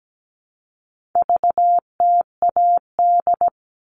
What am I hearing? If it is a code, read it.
Morse code, heard as a keyed tone, about 17 wpm, 700 Hz. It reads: VTAD
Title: All Cut Numbers - 4-Digit 17wpm